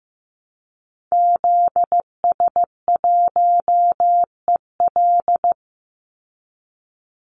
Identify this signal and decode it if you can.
Morse code, heard as a keyed tone, about 15 wpm, 700 Hz. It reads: ZS1EL
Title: Call Signs Repeated 2x - 15wpm